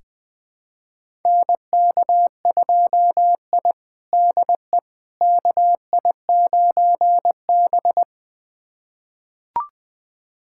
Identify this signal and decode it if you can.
Morse code, heard as a keyed tone, about 21 wpm, 700 Hz. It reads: NK2I DE KI9B
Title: USA Callsign DE Callsign - 4 Characters - 20wpm